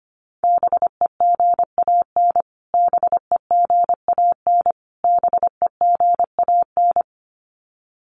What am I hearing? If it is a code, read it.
Morse code, heard as a keyed tone, about 25 wpm, 700 Hz. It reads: BEGAN BEGAN BEGAN
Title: Top 300 Words Repeated 25wpm